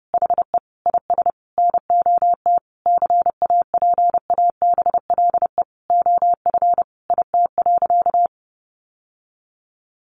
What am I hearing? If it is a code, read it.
Morse code, heard as a keyed tone, about 30 wpm, 700 Hz. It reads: HE IS NOT CAPABLE OF IT.
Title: Sentences from Top 1200 Words 30wpm